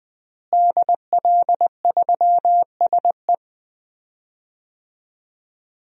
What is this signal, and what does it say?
Morse code, heard as a keyed tone, about 20 wpm, 700 Hz. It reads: DL3SE
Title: DXCC-100 Call Signs - 5 Characters - Repeated 2x - 20wpm